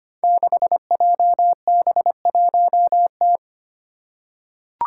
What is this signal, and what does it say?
Morse code, heard as a keyed tone, about 25 wpm, 700 Hz. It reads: 6JB1T
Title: Five Challenging Letter-Number Combinations 25wpm